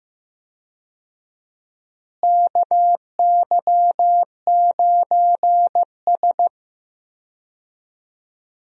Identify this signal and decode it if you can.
Morse code, heard as a keyed tone, about 15 wpm, 700 Hz. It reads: KY9S
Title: USA Call Signs - 4 Characters - Repeated 2x - 15wpm